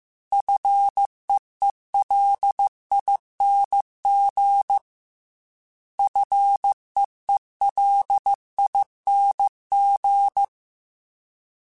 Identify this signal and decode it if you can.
Morse code, heard as a keyed tone, about 15 wpm, 800 Hz. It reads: FEELING FEELING